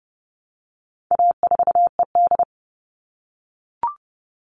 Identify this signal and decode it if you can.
Morse code, heard as a keyed tone, about 30 wpm, 700 Hz. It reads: A4ED
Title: All Cut Numbers - 4-Digit 30wpm